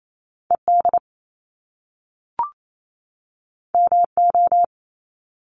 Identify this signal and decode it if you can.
Morse code, heard as a keyed tone, about 28 wpm, 700 Hz. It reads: ED MO
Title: Top Two Letter Combinations in Words 28wpm